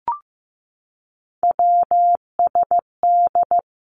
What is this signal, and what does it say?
Morse code, heard as a keyed tone, about 15 wpm, 700 Hz. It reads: WSD
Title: Three Challenging Letter-Number Combinations 15wpm